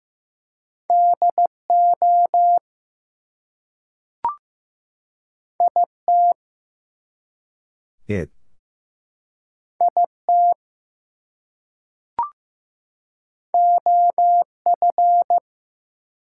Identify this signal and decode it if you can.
Morse code, heard as a keyed tone, about 15 wpm, 700 Hz. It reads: DO IT IT OF
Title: Two Letter Words - 15wpm